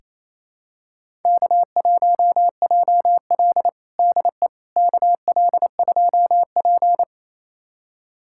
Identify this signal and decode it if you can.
Morse code, heard as a keyed tone, about 28 wpm, 700 Hz. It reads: K1JL DE KL2P
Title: USA Callsign DE Callsign - 4 Characters - 28wpm